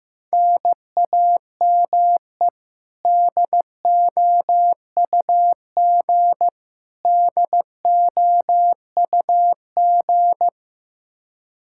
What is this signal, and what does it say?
Morse code, heard as a keyed tone, about 15 wpm, 700 Hz. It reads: NAME DOUG DOUG
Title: Practice QSO - Exchange 1 - Beginning - 15wpm